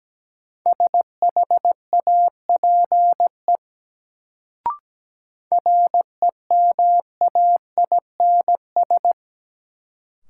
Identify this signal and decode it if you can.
Morse code, heard as a keyed tone, about 17 wpm, 700 Hz. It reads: SHAPE REMAINS
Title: Sets of 5 Words Encore - New Word Drill - 17wpm